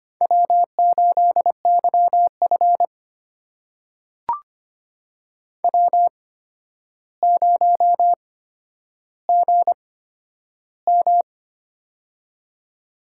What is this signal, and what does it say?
Morse code, heard as a keyed tone, about 25 wpm, spaced as for 11 wpm, 700 Hz. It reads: W8YF W0GM
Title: USA Call Signs - 4 Characters - 8x Character Spacing - 25wpm